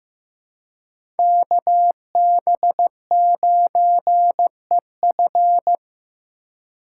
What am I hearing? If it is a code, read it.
Morse code, heard as a keyed tone, about 16 wpm, 700 Hz. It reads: KB9EF
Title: USA Call Signs - 3x Character Spacing - 15wpm